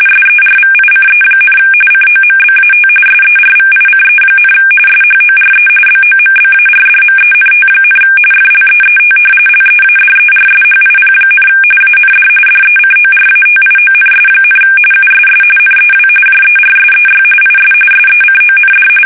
RTTY, about 45.45 baud. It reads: SERVICE, COMBAT SERVICE, SHORE BOMBARDMENT, FLEET LEADERSHIP, AND DECORATIONS, USS NEW JERSEY IS SECOND
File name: fsk850 / battleshipnj